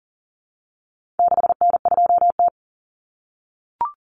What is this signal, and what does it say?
Morse code, heard as a keyed tone, about 40 wpm, 700 Hz. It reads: 6N2T
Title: Common Cut Numbers - 4-Digit 40wpm